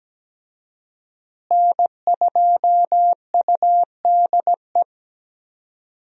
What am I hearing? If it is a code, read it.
Morse code, heard as a keyed tone, about 17 wpm, 700 Hz. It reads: N2UDE
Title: USA Call Signs - 5 Characters - Repeated 3x - 17wpm